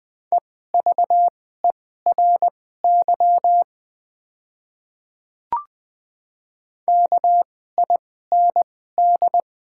Morse code, heard as a keyed tone, about 20 wpm, spaced as for 14 wpm, 700 Hz. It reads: EVERY KIND
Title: 2x Character Spacing - Top 500 Words - 20wpm